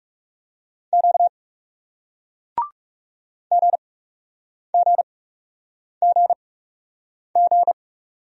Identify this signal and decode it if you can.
Morse code, heard as a keyed tone, about 38 wpm, spaced as for 8 wpm, 700 Hz. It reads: Q GGGG